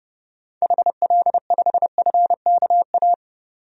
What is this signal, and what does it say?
Morse code, heard as a keyed tone, about 30 wpm, 700 Hz. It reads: HL5FKA
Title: DXCC-100 Call Signs - 6 Characters - 30wpm